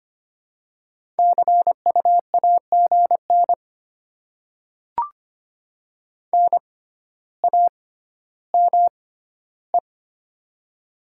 Morse code, heard as a keyed tone, about 25 wpm, spaced as for 13 wpm, 700 Hz. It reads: CU AGN NAME
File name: Top 20 QSO Elements - 6x Character Spacing - 25wpm